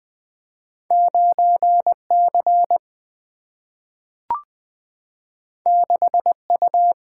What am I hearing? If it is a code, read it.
Morse code, heard as a keyed tone, about 20 wpm, 700 Hz. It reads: 9 C 6 U
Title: Single-Letter Single-Number 20wpm